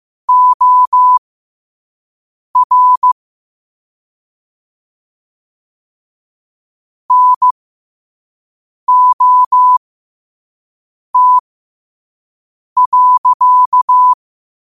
Morse code, heard as a keyed tone, about 15 wpm, spaced as for 5 wpm, 1000 Hz. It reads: OR NOT.
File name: qotdfiles / 5wpm